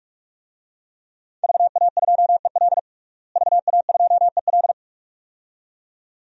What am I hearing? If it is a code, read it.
Morse code, heard as a keyed tone, about 46 wpm, 700 Hz. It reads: VA2EL VA2EL
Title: DXCC-100 Call Signs - 5 Characters - Repeated 2x - 45wpm